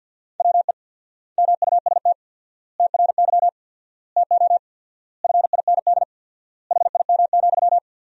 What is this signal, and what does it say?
Morse code, heard as a keyed tone, about 50 wpm, 700 Hz. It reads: WE MUST TRY TO FIND HIM,
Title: 4x Word-Spacing - Sentences from Top 400 Words - 50wpm